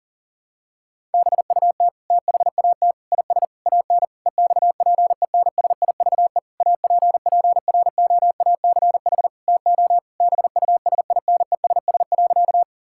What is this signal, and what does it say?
Morse code, heard as a keyed tone, about 40 wpm, 700 Hz. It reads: BUT THAT IS AN EXPENSIVE APPROACH TO BUSINESS.
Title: Sentences from Top 1000 Words - Review Part 6 - 40wpm